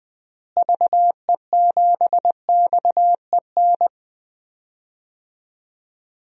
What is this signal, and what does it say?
Morse code, heard as a keyed tone, about 20 wpm, 700 Hz. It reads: VE7XEN
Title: DXCC-100 Call Signs - 6 Characters - Repeated 2x - 20wpm